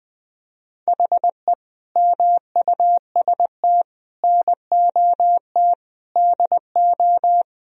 Morse code, heard as a keyed tone, about 20 wpm, 700 Hz. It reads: HE MUST NOT DO
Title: Sentences from Top 1000 Words - Review Part 2 - 20wpm